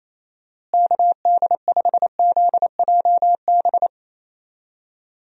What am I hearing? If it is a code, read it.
Morse code, heard as a keyed tone, about 27 wpm, 700 Hz. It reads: KD5ZJB
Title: USA Call Signs - 6 Characters - Repeated 3x - 28wpm